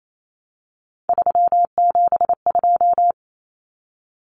Morse code, heard as a keyed tone, about 28 wpm, 700 Hz. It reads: 372
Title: Common Cut Numbers - 3-Digit 28wpm